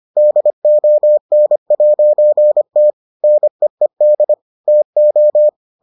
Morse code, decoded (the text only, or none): DON'T NEED TO